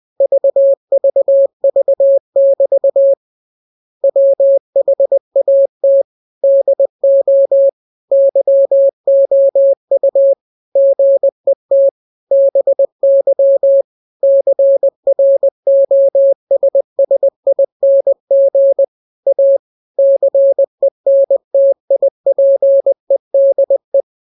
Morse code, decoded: VVV<BT> WHAT DO YOU GET BY CROSSING A CENTIPEDE